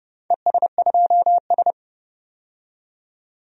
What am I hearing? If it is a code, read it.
Morse code, heard as a keyed tone, about 30 wpm, 700 Hz. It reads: ES2S
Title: DXCC-100 Call Signs - 4 Characters - 30wpm